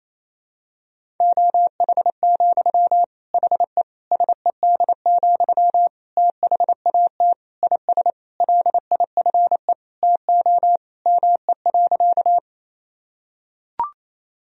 Morse code, heard as a keyed tone, about 28 wpm, 700 Hz. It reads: OH, HE SED, THAT IS LIFE TO ME.
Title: Sentences from Top 500 Words and Abbreviations 28wpm